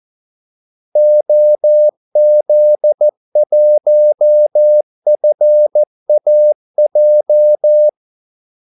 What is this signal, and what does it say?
Morse code, heard as a keyed tone, about 14 wpm, 600 Hz. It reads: OZ1FAJ
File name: Callsigns-14wpm